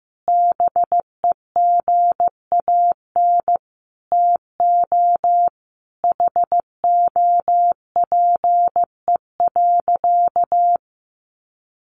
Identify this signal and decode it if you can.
Morse code, heard as a keyed tone, about 15 wpm, 700 Hz. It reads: BEGAN TO HOPE.
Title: Sentences from Top 1100 Words 15wpm